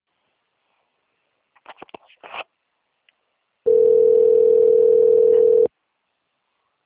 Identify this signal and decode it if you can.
Ringback tone (North America), 440 + 480 Hz.
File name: out- - - - - .32122